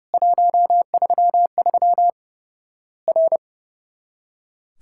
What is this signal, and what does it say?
Morse code, heard as a keyed tone, about 29 wpm, 700 Hz. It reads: 133 R
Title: Worked All Europe DX Contest - QTC Exchange - 30wpm